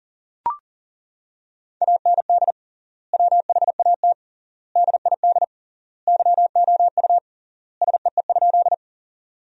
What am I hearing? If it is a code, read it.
Morse code, heard as a keyed tone, about 40 wpm, 700 Hz. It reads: AND WHAT DID YOU SEE?
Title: 3x Word-Spacing - Sentences from Top 100 Words - 40wpm